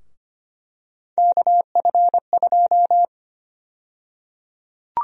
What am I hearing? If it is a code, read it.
Morse code, heard as a keyed tone, about 25 wpm, 700 Hz. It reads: KF2
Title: USA Call Sign Prefixes - 25wpm